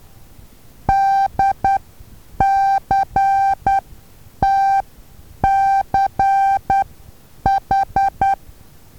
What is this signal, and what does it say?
Morse code, heard as a keyed tone, about 10 wpm, spaced as for 8 wpm, 780 Hz. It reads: DCTCH
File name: Taviro HA6VA A